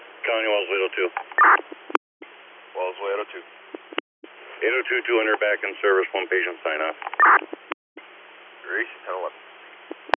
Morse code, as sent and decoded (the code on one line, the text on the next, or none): none